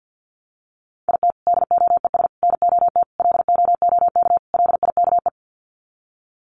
Morse code, heard as a keyed tone, about 50 wpm, 700 Hz. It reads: IT DOES NOT LOOK LIKE
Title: Sets of 5 Words Encore2 50wpm